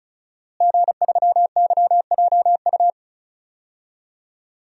Morse code, heard as a keyed tone, about 36 wpm, 700 Hz. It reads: G3YJU
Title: DXCC-100 Call Signs - 5 Characters - Repeated 3x - 35wpm